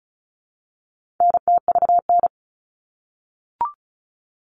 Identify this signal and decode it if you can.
Morse code, heard as a keyed tone, about 34 wpm, 700 Hz. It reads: NTVN
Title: All Cut Numbers - 4-Digit 35wpm